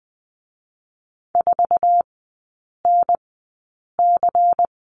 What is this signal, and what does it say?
Morse code, heard as a keyed tone, about 20 wpm, 700 Hz. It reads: 4 N C
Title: 100 Lesson 17e - Letter u - Three Characters 20wpm